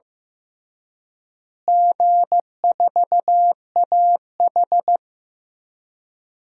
Morse code, heard as a keyed tone, about 15 wpm, 700 Hz. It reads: G4AH
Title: DXCC-100 Call Signs - 4 Characters - Repeated 3x - 15wpm